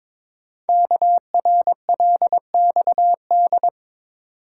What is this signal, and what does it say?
Morse code, heard as a keyed tone, about 22 wpm, 700 Hz. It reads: KRLXD
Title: Code Groups - 5 Letters - 22wpm